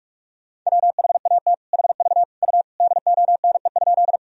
Morse code, heard as a keyed tone, about 42 wpm, 690 Hz. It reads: WHAT HV U DONE?
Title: Sentences from Top 500 Words and Abbreviations 45wpm